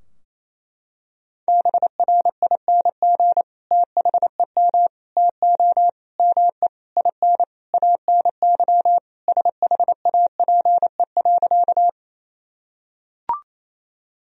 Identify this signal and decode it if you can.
Morse code, heard as a keyed tone, about 28 wpm, 700 Hz. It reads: BRING THEM TO ME IN ANY SHAPE.
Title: Sentences from Top 1000 Words - Review Part 3 - 28wpm